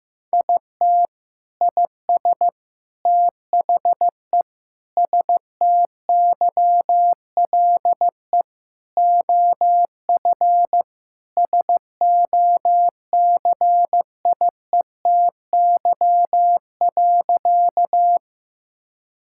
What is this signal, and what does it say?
Morse code, heard as a keyed tone, about 15 wpm, 700 Hz. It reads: IT IS THE STYLE OF SOCIETY.
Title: Sentences from Top 1000 Words - Review Part 5 - 15wpm